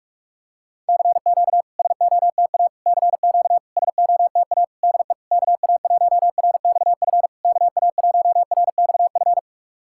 Morse code, heard as a keyed tone, about 44 wpm, 660 Hz. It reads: CQ SOTA CQ SOTA DE KA1RXF KA1RXF
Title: SOTA QSO Practice 45wpm